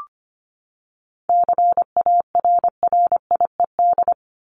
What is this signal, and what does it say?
Morse code, heard as a keyed tone, about 25 wpm, 700 Hz. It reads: CARRIED
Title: ED Suffix - Top 100 Words with Base Words 25wpm